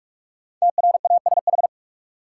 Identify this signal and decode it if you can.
Morse code, heard as a keyed tone, about 46 wpm, 700 Hz. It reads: TRASH
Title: Five Letter Words - 45wpm